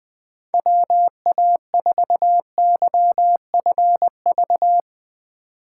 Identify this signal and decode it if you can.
Morse code, heard as a keyed tone, about 20 wpm, 700 Hz. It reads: WA4YFV
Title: USA Call Signs - 6 Characters - 8x Character Spacing - 20wpm